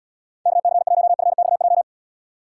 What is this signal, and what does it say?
Morse code, heard as a keyed tone, about 76 wpm, 700 Hz. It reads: KC1FXJ